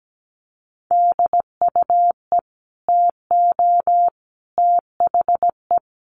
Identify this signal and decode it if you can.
Morse code, heard as a keyed tone, about 17 wpm, 700 Hz. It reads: DUE TO THE
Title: Sets of 5 Words Encore 17wpm